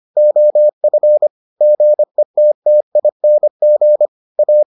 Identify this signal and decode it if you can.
Morse code, heard as a keyed tone, about 25 wpm, 600 Hz. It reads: OF GETTING A